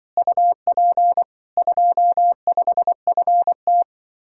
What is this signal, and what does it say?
Morse code, heard as a keyed tone, about 23 wpm, 700 Hz. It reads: UP 25FT
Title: Speed-Racing Common QSO Phrases Encore - 20wpm